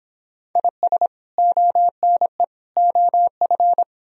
Morse code, heard as a keyed tone, about 28 wpm, 700 Hz. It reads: IS ONE OF